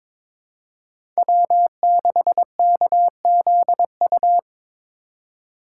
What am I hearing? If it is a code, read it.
Morse code, heard as a keyed tone, about 21 wpm, 700 Hz. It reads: W6KZU
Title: USA Call Signs - 5 Characters - 2x Character Spacing - 22wpm